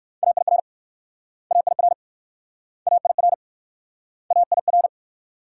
Morse code, heard as a keyed tone, about 64 wpm, 700 Hz. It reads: AIR AIR AIR AIR